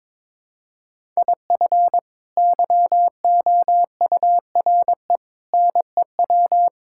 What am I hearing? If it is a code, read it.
Morse code, heard as a keyed tone, about 22 wpm, 700 Hz. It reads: IF YOURE NEW
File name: practice / 22wpm / Sets of 8 Words - 22wpm